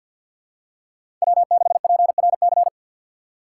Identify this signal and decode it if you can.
Morse code, heard as a keyed tone, about 53 wpm, 700 Hz. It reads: W6PRC
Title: USA Call Signs - 6x Character Spacing - 50wpm